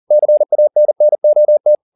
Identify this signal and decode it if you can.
Morse code, heard as a keyed tone, about 40 wpm, 600 Hz. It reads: CANNOT